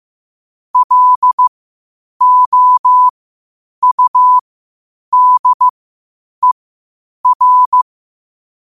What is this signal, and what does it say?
Morse code, heard as a keyed tone, about 15 wpm, spaced as for 8 wpm, 1000 Hz. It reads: LOUDER
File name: qotdfiles / 8wpm